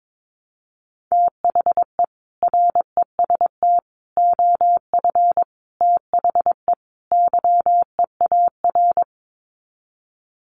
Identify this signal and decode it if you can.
Morse code, heard as a keyed tone, about 22 wpm, 700 Hz. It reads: THE REST OF THE YEAR
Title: Sets of 5 Words Encore 22wpm